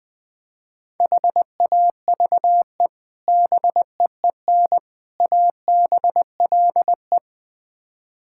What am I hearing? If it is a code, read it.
Morse code, heard as a keyed tone, about 20 wpm, 700 Hz. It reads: HAVE BEEN ABLE
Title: Build A Phrase - 4-Word Phrases - 20wpm